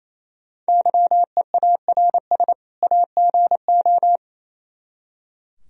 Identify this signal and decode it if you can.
Morse code, heard as a keyed tone, about 34 wpm, 700 Hz. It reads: YEARS AGO
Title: Speed-Racing - Sets of 2 Words 28wpm